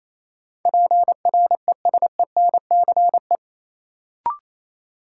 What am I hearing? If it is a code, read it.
Morse code, heard as a keyed tone, about 28 wpm, 700 Hz. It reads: PRESENCE
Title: Sets of 3 Words - New Word Drill - 28wpm